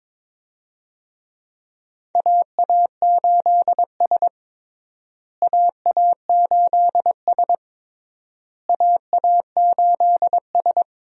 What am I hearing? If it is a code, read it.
Morse code, heard as a keyed tone, about 22 wpm, 700 Hz. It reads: AA8S AA8S AA8S
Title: USA Call Signs - 4 Characters - Repeated 3x - 22wpm